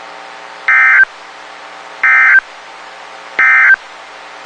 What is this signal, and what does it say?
Emergency Alert System (SAME) tones, heard as harsh buzzing bursts. It is heard from 0.6 to 3.8 s.
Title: WXK91 NOAA Weather Radio